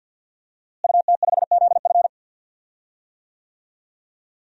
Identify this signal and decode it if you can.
Morse code, heard as a keyed tone, about 50 wpm, 700 Hz. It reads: UT5ZF
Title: DXCC-100 Call Signs - 5 Characters - Repeated 3x - 50wpm